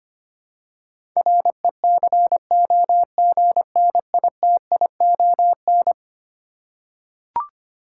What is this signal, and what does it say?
Morse code, heard as a keyed tone, about 25 wpm, 700 Hz. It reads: RECOGNITION
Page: morse-code-ninja-practice-25wpm